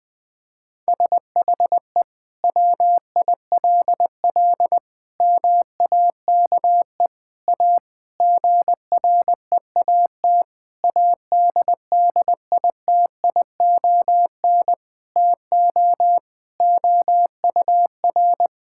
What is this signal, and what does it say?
Morse code, heard as a keyed tone, about 20 wpm, 700 Hz. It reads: SHE WILL MAKE A GREAT ADDITION TO OUR
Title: Sentences from Top 1000 Words - Review Part 6 - 20wpm